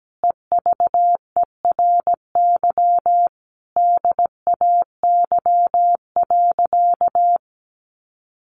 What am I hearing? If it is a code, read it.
Morse code, heard as a keyed tone, about 17 wpm, 700 Hz. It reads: EVERY DAY.
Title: Sentences from Top 200 Words 17wpm